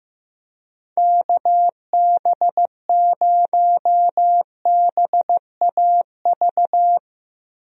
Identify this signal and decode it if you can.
Morse code, heard as a keyed tone, about 15 wpm, 700 Hz. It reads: KB0BAV
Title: USA Call Signs - 6 Characters - Repeated 2x - 15wpm